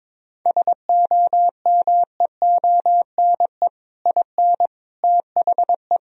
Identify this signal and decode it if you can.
Morse code, heard as a keyed tone, about 22 wpm, 700 Hz. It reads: SOMEONE IN THE
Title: Sentences from Top 1000 Words - Review Part 1 - 22wpm